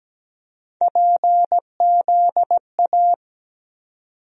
Morse code, heard as a keyed tone, about 17 wpm, 700 Hz. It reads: PZA
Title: USA Call Sign Postfixes - 3 Letters 17wpm